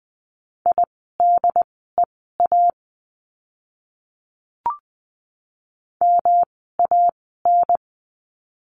Morse code, heard as a keyed tone, about 20 wpm, spaced as for 15 wpm, 700 Hz. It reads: IDEA MAN
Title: 108 Lesson 18f - Letter m - Words 20wpm